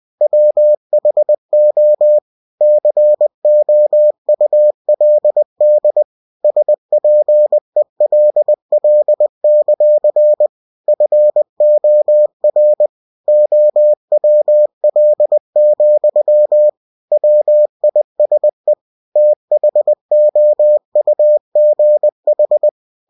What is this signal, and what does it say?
Morse code, heard as a keyed tone, about 20 wpm, 600 Hz. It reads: WHO COULD SPELL; FOR OWL, WISE THOUGH
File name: cw-books / winnie-the-pooh / winnie-20wpm-page-0029